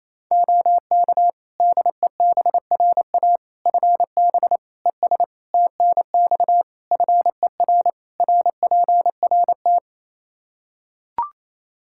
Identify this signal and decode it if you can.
Morse code, heard as a keyed tone, about 28 wpm, 700 Hz. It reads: OK DEBRA FB ES TNX FER RPRT
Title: Practice QSO - Exchange 2 - Info - 28wpm